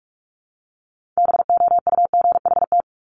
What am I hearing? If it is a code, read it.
Morse code, heard as a keyed tone, about 46 wpm, 700 Hz. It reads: BOUGHT
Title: Multi-letter Phonemes 45wpm